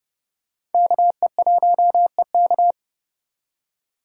Morse code, heard as a keyed tone, about 30 wpm, 700 Hz. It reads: KE1EK
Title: USA Call Signs - 5 Characters - 2x Character Spacing - 30wpm